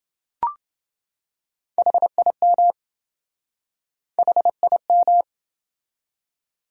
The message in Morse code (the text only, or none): HIM HIM